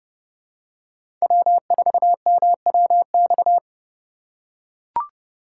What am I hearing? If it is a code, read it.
Morse code, heard as a keyed tone, about 27 wpm, 700 Hz. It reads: W4MWX